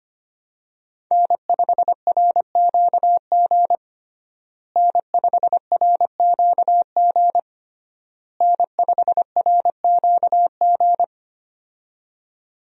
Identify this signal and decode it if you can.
Morse code, heard as a keyed tone, about 25 wpm, 700 Hz. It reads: N5RQG N5RQG N5RQG
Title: USA Call Signs - 5 Characters - Repeated 3x - 25wpm